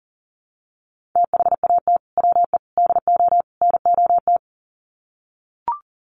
Morse code, heard as a keyed tone, about 40 wpm, 700 Hz. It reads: THAT WE DO NOT
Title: Sets of 4 Words 40wpm